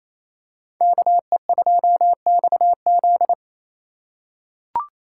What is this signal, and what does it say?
Morse code, heard as a keyed tone, about 28 wpm, 700 Hz. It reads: KE2XZ
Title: USA Call Signs - 4x Character Spacing - 28wpm